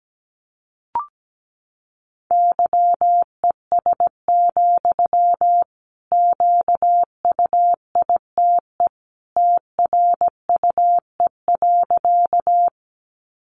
Morse code, heard as a keyed tone, about 17 wpm, 700 Hz. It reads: YES, QUITE TRUE.
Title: Sentences from Top 700 Words 17wpm